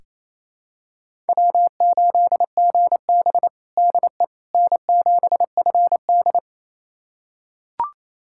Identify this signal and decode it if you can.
Morse code, heard as a keyed tone, about 28 wpm, 700 Hz. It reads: W8GB DE N7FD
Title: USA Callsign DE Callsign - 4 Characters - 28wpm